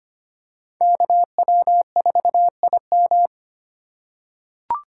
Morse code, heard as a keyed tone, about 25 wpm, 700 Hz. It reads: KW4IM
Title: USA Call Signs - 5 Characters - 3x Character Spacing - 25wpm